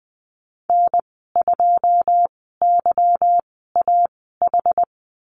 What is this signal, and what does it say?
Morse code, heard as a keyed tone, about 19 wpm, spaced as for 14 wpm, 700 Hz. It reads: N2YAH
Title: 179 Lesson 29g - Slash - Call Signs 20wpm